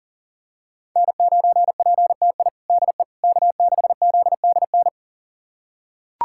Morse code, heard as a keyed tone, about 37 wpm, 700 Hz. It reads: N9PTI DE K6ZDN
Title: USA Callsign DE Callsign - 5 Characters - 40wpm